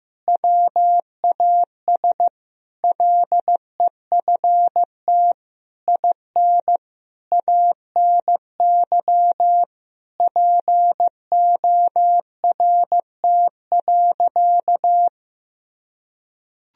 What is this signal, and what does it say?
Morse code, heard as a keyed tone, about 15 wpm, 700 Hz. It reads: WAS LEFT IN ANY PORT.
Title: Sentences from Top 1000 Words - Review Part 2 - 15wpm